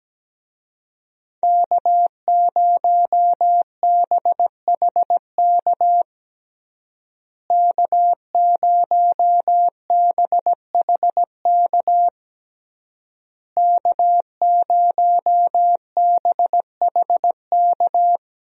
Morse code, heard as a keyed tone, about 17 wpm, 700 Hz. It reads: K0BHK K0BHK K0BHK
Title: USA Call Signs - 5 Characters - Repeated 3x - 17wpm